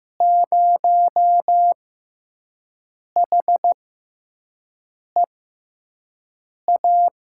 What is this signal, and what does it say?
Morse code, heard as a keyed tone, about 15 wpm, spaced as for 5 wpm, 700 Hz. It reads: 0HEA
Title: USA Call Signs - 6 Characters - 6x Character Spacing - 15wpm